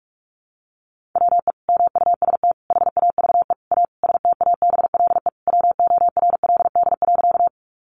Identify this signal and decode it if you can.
Morse code, heard as a keyed tone, about 46 wpm, 700 Hz. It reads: WE MUST HAVE A STABLE WORLD.
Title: Sentences from Top 1500 Words 45wpm